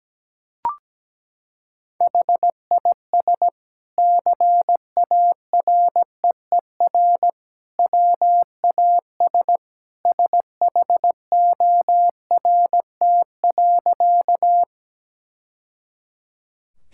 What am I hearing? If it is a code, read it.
Morse code, heard as a keyed tone, about 17 wpm, 700 Hz. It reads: HIS CAREER WAS SHORT.
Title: Sentences from Top 1000 Words - Review Part 6 - 17wpm